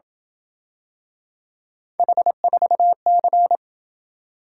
Morse code, heard as a keyed tone, about 27 wpm, 700 Hz. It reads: H4C